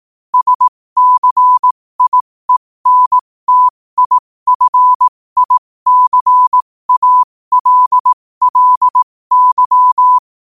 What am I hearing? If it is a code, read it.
Morse code, heard as a keyed tone, about 18 wpm, spaced as for 15 wpm, 1000 Hz. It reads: SCIENTIFICALLY
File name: qotdfiles / 15wpm